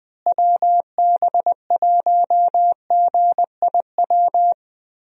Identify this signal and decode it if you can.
Morse code, heard as a keyed tone, about 20 wpm, 700 Hz. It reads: WB1GIW
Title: USA Call Signs - 6 Characters - Repeated 3x - 20wpm